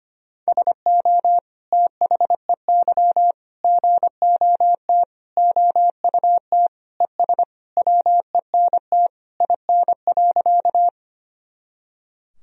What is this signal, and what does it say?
Morse code, heard as a keyed tone, about 25 wpm, 700 Hz. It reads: SO THEY GOT OUT ES WENT IN.
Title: Sentences from Top 300 Words and Abbreviations 25wpm